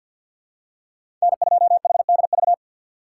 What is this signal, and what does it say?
Morse code, heard as a keyed tone, about 46 wpm, 700 Hz. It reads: N2HDV
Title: USA Call Signs - 5 Characters - 4x Character Spacing - 50wpm